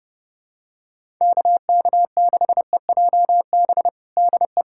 Morse code, heard as a keyed tone, about 32 wpm, 700 Hz. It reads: KK6EJB DE
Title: USA Callsign DE Callsign - 6 Characters - 30wpm